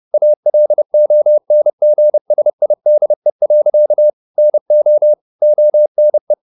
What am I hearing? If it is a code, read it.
Morse code, heard as a keyed tone, about 30 wpm, 600 Hz. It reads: ALONGSIDE. NO ONE
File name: the old man and the sea / the old man and the sea 1 30wpm